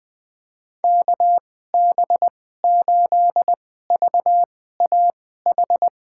Morse code, heard as a keyed tone, about 20 wpm, spaced as for 14 wpm, 700 Hz. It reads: KB8VAH
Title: USA Call Signs - 6 Characters - 2x Character Spacing - 20wpm